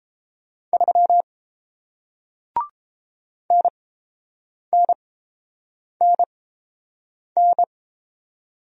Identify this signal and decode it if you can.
Morse code, heard as a keyed tone, about 30 wpm, spaced as for 7 wpm, 700 Hz. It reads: 3 NNNN